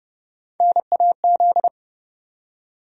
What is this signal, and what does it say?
Morse code, heard as a keyed tone, about 30 wpm, 700 Hz. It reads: NAZ
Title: USA Call Sign Postfixes - 3 Letters 30wpm